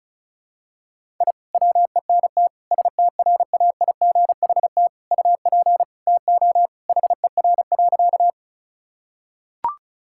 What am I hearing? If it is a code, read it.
Morse code, heard as a keyed tone, about 34 wpm, 700 Hz. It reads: I WENT STRAIGHT UP TO HER.
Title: Sentences from Top 1000 Words - Review Part 6 - 35wpm